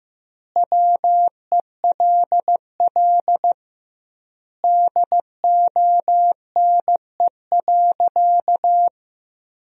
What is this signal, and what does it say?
Morse code, heard as a keyed tone, about 16 wpm, 700 Hz. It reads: WELL DONE.
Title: 2x Word-Spacing - Sentences from Top 500 Words - 15wpm